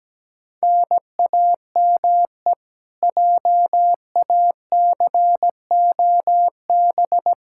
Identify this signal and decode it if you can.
Morse code, heard as a keyed tone, about 17 wpm, 700 Hz. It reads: NAME JACOB
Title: K1USN SST QSO Practice 17wpm